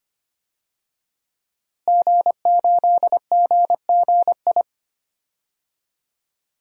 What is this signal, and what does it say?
Morse code, heard as a keyed tone, about 25 wpm, 700 Hz. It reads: G8GGI
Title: DXCC-100 Call Signs - 5 Characters - Repeated 2x - 25wpm